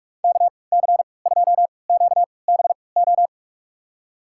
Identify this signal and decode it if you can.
Morse code, heard as a keyed tone, about 46 wpm, spaced as for 25 wpm, 700 Hz. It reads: KC2QBO
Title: USA Call Signs - 3x Character Spacing - 45wpm